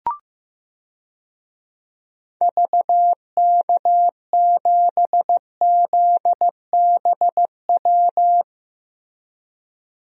Morse code, heard as a keyed tone, about 15 wpm, 700 Hz. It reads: VK7ZBW
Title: DXCC-100 Call Signs - 6 Characters - Repeated 2x - 15wpm